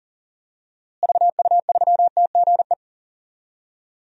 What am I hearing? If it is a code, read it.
Morse code, heard as a keyed tone, about 38 wpm, 700 Hz. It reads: VU3TGE
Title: DXCC-100 Call Signs - 6 Characters - Repeated 2x - 40wpm